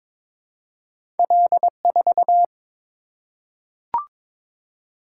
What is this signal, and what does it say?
Morse code, heard as a keyed tone, about 22 wpm, 700 Hz. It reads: L4
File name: practice / 22wpm / Two Letter-Number Combinations - 22wpm